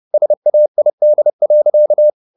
Morse code, heard as a keyed tone, about 30 wpm, 600 Hz. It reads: SAID.